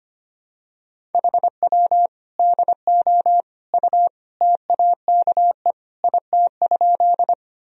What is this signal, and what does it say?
Morse code, heard as a keyed tone, about 25 wpm, 700 Hz. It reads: HW DO U TAKE IT?
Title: Sentences from Top 200 Words and Abbreviations 25wpm